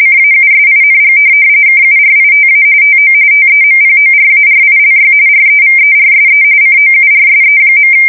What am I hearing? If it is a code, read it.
RTTY, about 45.45 baud. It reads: CAUSING KURITA TO TEMPORARILY REVERSE COURSE.